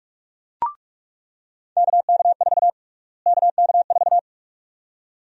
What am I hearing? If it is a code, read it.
Morse code, heard as a keyed tone, about 47 wpm, 700 Hz. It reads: KK4 KK4
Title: USA Call Sign Prefixes - Repeated 2x - 45wpm